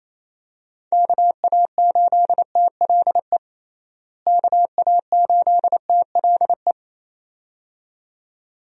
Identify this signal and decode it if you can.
Morse code, heard as a keyed tone, about 28 wpm, 700 Hz. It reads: KA8TLE KA8TLE
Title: USA Call Signs - 6 Characters - Repeated 2x - 28wpm